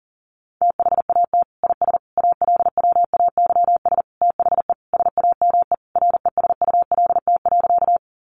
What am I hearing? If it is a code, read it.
Morse code, heard as a keyed tone, about 40 wpm, 700 Hz. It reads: THAT IS ALWAYS THE SAME RESULT.
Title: Sentences from Top 700 Words 40wpm